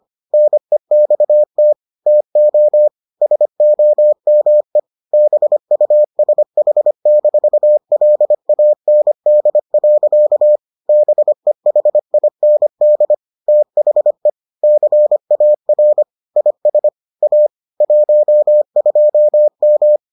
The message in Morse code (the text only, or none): NEXT TO SOME BUSH-LAND. BEHIND THE CAR IS A 12M